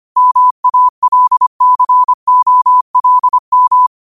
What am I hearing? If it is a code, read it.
Morse code, heard as a keyed tone, about 25 wpm, 1000 Hz. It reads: MALCOLM
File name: qotdfiles / 25wpm